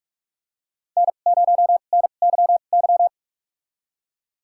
Morse code, heard as a keyed tone, about 46 wpm, spaced as for 33 wpm, 700 Hz. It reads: N0NYY
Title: USA Call Signs - 5 Characters - 2x Character Spacing - 45wpm